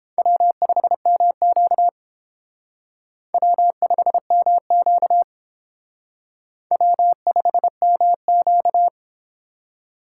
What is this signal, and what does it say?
Morse code, heard as a keyed tone, about 29 wpm, 700 Hz. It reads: W5MQ W5MQ W5MQ